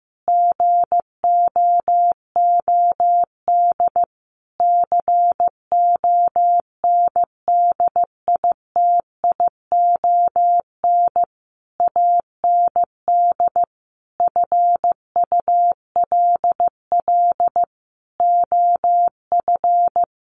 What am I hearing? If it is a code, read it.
Morse code, heard as a keyed tone, about 15 wpm, 700 Hz. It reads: GOOD CONDITION AND FULL OF
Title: Sentences from Top 1500 Words 15wpm